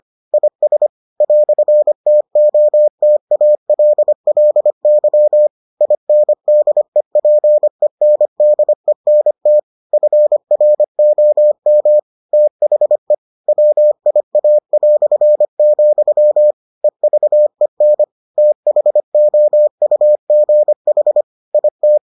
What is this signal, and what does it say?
Morse code, heard as a keyed tone, about 25 wpm, 600 Hz. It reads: IS "TOTALLY INDEPENDENT FROM THE WIA", EVEN THOUGH IT